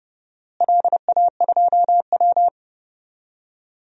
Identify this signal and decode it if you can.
Morse code, heard as a keyed tone, about 30 wpm, 700 Hz. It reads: LA2W
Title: DXCC-100 Call Signs - 4 Characters - Repeated 3x - 30wpm